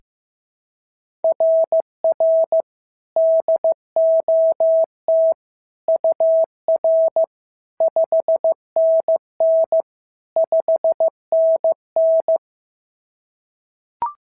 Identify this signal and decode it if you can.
Morse code, heard as a keyed tone, about 14 wpm, 650 Hz. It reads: RR DOT UR 5NN 5NN
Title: SOTA QSO Practice 15wpm